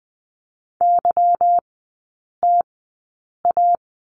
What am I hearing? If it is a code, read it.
Morse code, heard as a keyed tone, about 20 wpm, spaced as for 8 wpm, 700 Hz. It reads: YTA
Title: 184 Lesson 30e - Letter b - Three Characters 20wpm